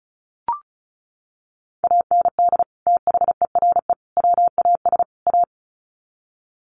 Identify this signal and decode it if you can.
Morse code, heard as a keyed tone, about 34 wpm, 700 Hz. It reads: AND THERE WAS A
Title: Sets of 4 Words 35wpm